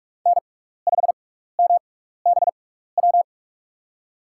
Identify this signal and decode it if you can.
Morse code, heard as a keyed tone, about 43 wpm, spaced as for 15 wpm, 700 Hz. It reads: N5MBW
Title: USA Call Signs - 5 Characters - 6x Character Spacing - 45wpm